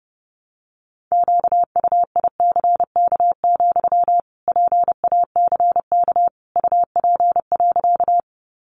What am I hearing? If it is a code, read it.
Morse code, heard as a keyed tone, about 30 wpm, 700 Hz. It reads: QUICK, PACK UP.
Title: Sentences from Top 1300 Words 30wpm